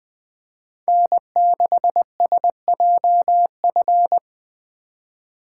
Morse code, heard as a keyed tone, about 21 wpm, 700 Hz. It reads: N6SJF